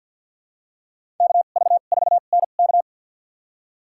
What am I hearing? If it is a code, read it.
Morse code, heard as a keyed tone, about 52 wpm, spaced as for 37 wpm, 700 Hz. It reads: KV4NK
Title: USA Call Signs - 5 Characters - 2x Character Spacing - 50wpm